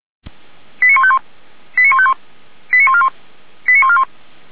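Five-tone selective call (ZVEI), heard as quick runs of five tones, tones of about 70 ms. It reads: 89151 89151 89141 89141
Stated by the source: ZVEI five-tone selective call 85371, received on an SDR.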